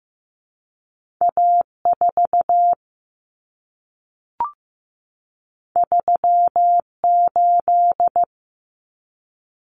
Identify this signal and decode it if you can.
Morse code, heard as a keyed tone, about 15 wpm, 700 Hz. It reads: A4 38
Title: Common Cut Numbers - 2-Digit 15wpm